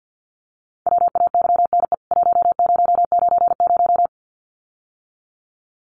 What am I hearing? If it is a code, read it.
Morse code, heard as a keyed tone, about 50 wpm, 700 Hz. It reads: WAYNE 1090
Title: CWT Exchange 50wpm